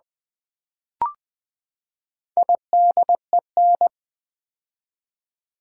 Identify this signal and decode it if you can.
Morse code, heard as a keyed tone, about 20 wpm, 700 Hz. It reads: IDEN
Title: Build A Word - 4 Letters At A Time - 8-Letter Words - 20wpm